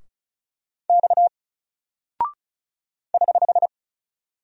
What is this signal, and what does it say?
Morse code, heard as a keyed tone, about 36 wpm, 700 Hz. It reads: X <HH>
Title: Single Character (Letter:Number:Punctuation:Prosign) Alt 35wpm